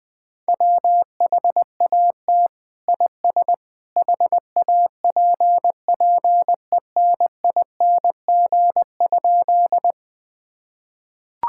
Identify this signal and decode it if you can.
Morse code, heard as a keyed tone, about 20 wpm, 700 Hz. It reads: WHAT IS HAPPENING?
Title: Sentences with 2-Word Phrases - 20wpm